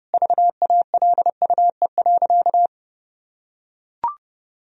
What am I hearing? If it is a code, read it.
Morse code, heard as a keyed tone, about 30 wpm, 700 Hz. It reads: VALUE.
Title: Sentences from Top 1000 Words - Review Part 4 - 30wpm